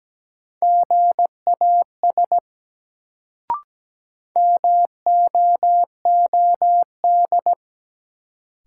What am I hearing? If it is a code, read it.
Morse code, heard as a keyed tone, about 17 wpm, 700 Hz. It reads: GAS MOOD
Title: Sentences from Top 1100 Words - New Word Drill Alt - 17wpm